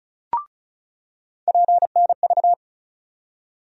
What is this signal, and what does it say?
Morse code, heard as a keyed tone, about 36 wpm, 700 Hz. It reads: PNV
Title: USA Call Sign Postfixes - 3 Letters 35wpm